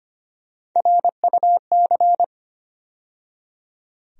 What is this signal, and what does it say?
Morse code, heard as a keyed tone, about 25 wpm, 700 Hz. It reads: RUC
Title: USA Call Sign Postfixes - 3 Letters 25wpm